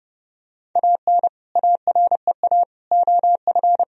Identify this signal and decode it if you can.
Morse code, heard as a keyed tone, about 30 wpm, 700 Hz. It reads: AN AREA OF
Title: Sentences from Top 1000 Words - Review Part 2 - 30wpm